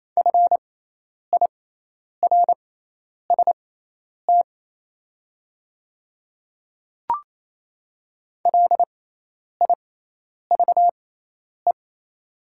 Morse code, heard as a keyed tone, about 28 wpm, spaced as for 10 wpm, 700 Hz. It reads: FIRST LIVE
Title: 6x Character Spacing - Top 200 Words - 28wpm